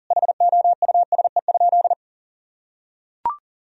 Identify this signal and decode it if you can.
Morse code, heard as a keyed tone, about 40 wpm, 700 Hz. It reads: HOUSE?
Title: Sentences from Top 1000 Words - Review Part 4 - 40wpm